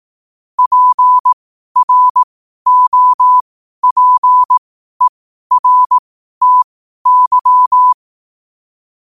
Morse code, heard as a keyed tone, about 18 wpm, spaced as for 12 wpm, 1000 Hz. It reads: PROPERTY
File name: qotdfiles / 12wpm